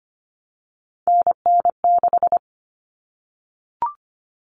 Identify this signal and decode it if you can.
Morse code, heard as a keyed tone, about 25 wpm, 700 Hz. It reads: NN6
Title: Common Cut Numbers - 3-Digit 25wpm